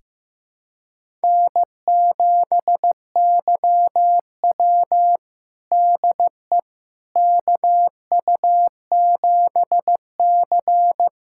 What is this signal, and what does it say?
Morse code, heard as a keyed tone, about 15 wpm, 700 Hz. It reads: N7YW DE KU7C
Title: USA Callsign DE Callsign - 4 Characters - 15wpm